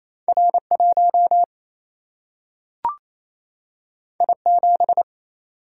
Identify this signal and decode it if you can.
Morse code, heard as a keyed tone, about 28 wpm, 700 Hz. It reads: R 1 I 7
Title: Single-Letter Single-Number 28wpm